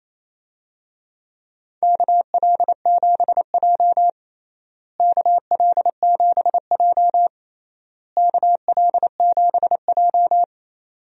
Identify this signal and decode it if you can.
Morse code, heard as a keyed tone, about 28 wpm, 700 Hz. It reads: KL7J KL7J KL7J
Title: DXCC-100 Call Signs - 4 Characters - Repeated 3x - 28wpm